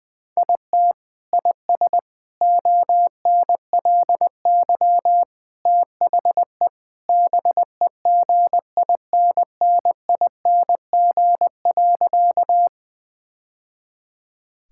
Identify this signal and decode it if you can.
Morse code, heard as a keyed tone, about 20 wpm, 700 Hz. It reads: IT IS ONLY THE BEGINNING.
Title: Sentences from Top 1000 Words - Review Part 5 - 20wpm